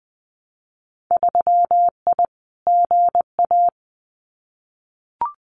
Morse code, heard as a keyed tone, about 20 wpm, 700 Hz. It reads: 3I GA
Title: Winter Field Day Exchanges 20wpm